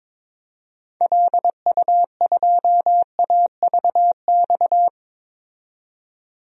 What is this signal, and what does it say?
Morse code, heard as a keyed tone, about 22 wpm, 700 Hz. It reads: LU2AVX
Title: DXCC-100 Call Signs - 6 Characters - 22wpm